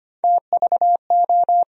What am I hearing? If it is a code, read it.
Morse code, heard as a keyed tone, about 25 wpm, 700 Hz. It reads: TVO
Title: USA Call Sign Postfixes - 3 Letters 25wpm